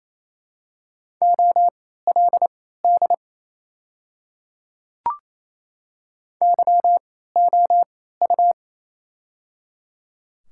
Morse code, heard as a keyed tone, about 28 wpm, spaced as for 16 wpm, 700 Hz. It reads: OLD YOU
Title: 3x Character Spacing - Top 500 Words - 28wpm